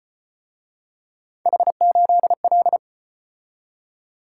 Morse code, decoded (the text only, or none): H8L